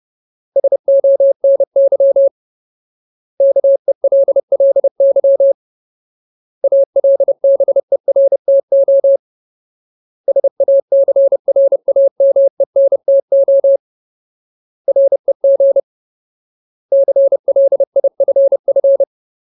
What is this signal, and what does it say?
Morse code, heard as a keyed tone, about 30 wpm, 550 Hz. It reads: SONY KELLY ALBERTO SACRAMENTO REG CLIFF
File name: Ultimate Word List / 30WPM Ultimate